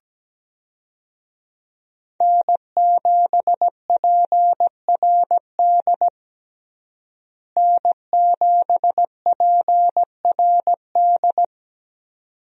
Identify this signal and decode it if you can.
Morse code, heard as a keyed tone, about 17 wpm, 700 Hz. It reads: N7PRD N7PRD
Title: USA Call Signs - 5 Characters - Repeated 2x - 17wpm